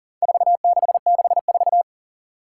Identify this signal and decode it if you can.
Morse code, heard as a keyed tone, about 37 wpm, 700 Hz. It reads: 4664
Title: Four Challenging Letter-Number Combinations 40wpm